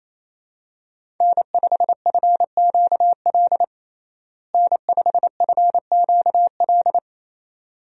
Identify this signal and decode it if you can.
Morse code, heard as a keyed tone, about 28 wpm, 700 Hz. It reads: N5FQL N5FQL
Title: USA Call Signs - 5 Characters - Repeated 2x - 28wpm